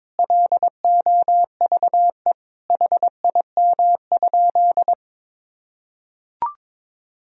Morse code, heard as a keyed tone, about 22 wpm, 700 Hz. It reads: LOVE HIM?
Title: Sentences from Top 400 Words and Abbreviations 22wpm